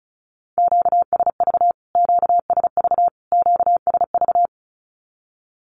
Morse code, heard as a keyed tone, about 34 wpm, 700 Hz. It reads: QSV QSV QSV
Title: Q-Codes Repeated 3x 35wpm